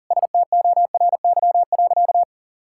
Morse code, heard as a keyed tone, about 40 wpm, 700 Hz. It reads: STORY.
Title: Sentences from Top 1000 Words - Review Part 5 - 40wpm